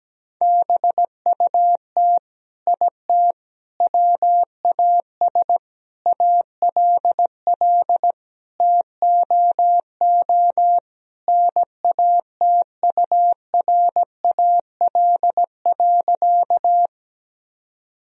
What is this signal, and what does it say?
Morse code, heard as a keyed tone, about 17 wpm, 700 Hz. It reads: BUT IT WAS ALL TOO NATURAL.
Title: Sentences from Top 1000 Words - Review Part 4 - 17wpm